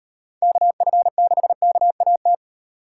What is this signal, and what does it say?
Morse code, heard as a keyed tone, about 38 wpm, 700 Hz. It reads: KF6KAT